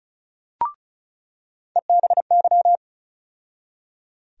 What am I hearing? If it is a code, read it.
Morse code, heard as a keyed tone, about 34 wpm, 700 Hz. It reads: EBY
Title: USA Call Sign Postfixes - 3 Letters 35wpm